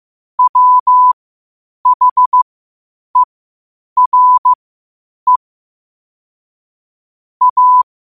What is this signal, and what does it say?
Morse code, heard as a keyed tone, about 15 wpm, spaced as for 8 wpm, 1000 Hz. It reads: WHERE A
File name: qotdfiles / 8wpm